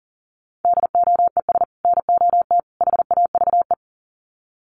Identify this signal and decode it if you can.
Morse code, heard as a keyed tone, about 40 wpm, 700 Hz. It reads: DOES NOT HAVE
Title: Sets of 3 Words 40wpm